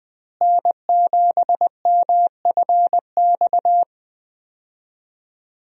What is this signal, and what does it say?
Morse code, heard as a keyed tone, about 20 wpm, 700 Hz. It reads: N7MFX
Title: USA Call Signs - 5 Characters - Repeated 2x - 20wpm